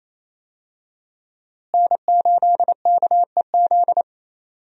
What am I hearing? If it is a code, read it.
Morse code, heard as a keyed tone, about 28 wpm, 700 Hz. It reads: N8KEZ
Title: USA Call Signs - 5 Characters - Repeated 2x - 28wpm